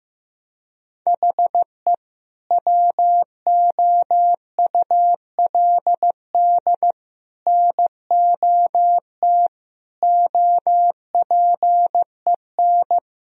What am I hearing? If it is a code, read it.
Morse code, heard as a keyed tone, about 15 wpm, 700 Hz. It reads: HE WOULD NOT OPEN
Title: Sentences from Top 1000 Words - Review Part 1 - 15wpm